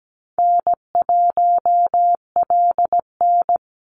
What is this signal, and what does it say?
Morse code, heard as a keyed tone, about 17 wpm, 700 Hz. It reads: N1LN
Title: Easy Call Signs Repeated 3x - 17wpm